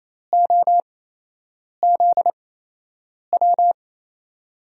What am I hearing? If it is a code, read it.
Morse code, heard as a keyed tone, about 28 wpm, spaced as for 8 wpm, 700 Hz. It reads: OZW
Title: USA Call Signs - 6 Characters - 8x Character Spacing - 28wpm